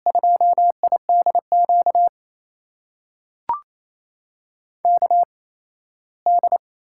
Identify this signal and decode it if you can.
Morse code, heard as a keyed tone, about 28 wpm, spaced as for 17 wpm, 700 Hz. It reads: 2IDQ KD
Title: USA Call Signs - 6 Characters - 8x Character Spacing - 28wpm